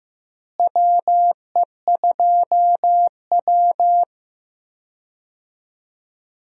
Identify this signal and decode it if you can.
Morse code, heard as a keyed tone, about 15 wpm, 700 Hz. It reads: WE2W
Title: USA Call Signs - 4 Characters - Repeated 2x - 15wpm